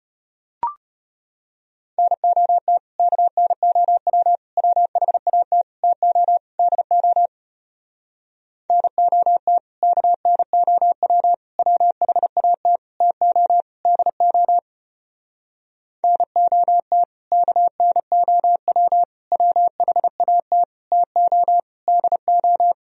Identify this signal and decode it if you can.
Morse code, heard as a keyed tone, about 33 wpm, 700 Hz. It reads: NOT KNOW WHAT TO DO NOT KNOW WHAT TO DO NOT KNOW WHAT TO DO